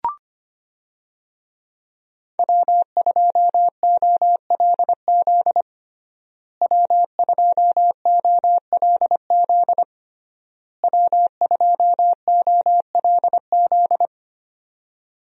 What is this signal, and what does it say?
Morse code, heard as a keyed tone, about 25 wpm, 700 Hz. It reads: W2OLZ W2OLZ W2OLZ
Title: USA Call Signs - 5 Characters - Repeated 3x - 25wpm